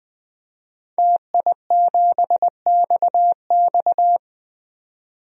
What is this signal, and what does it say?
Morse code, heard as a keyed tone, about 20 wpm, 700 Hz. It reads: TI7XX
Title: DXCC-100 Call Signs - 5 Characters - 20wpm